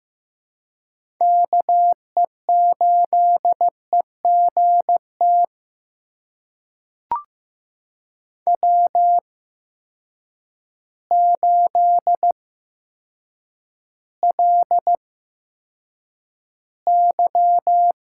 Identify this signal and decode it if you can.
Morse code, heard as a keyed tone, about 15 wpm, spaced as for 8 wpm, 700 Hz. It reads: KE8EGT W8LY
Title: USA Call Signs - 8x Character Spacing - 15wpm